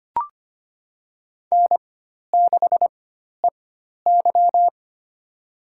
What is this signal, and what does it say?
Morse code, heard as a keyed tone, about 25 wpm, spaced as for 12 wpm, 700 Hz. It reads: N6EY
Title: USA Call Signs - 4x Character Spacing - 25wpm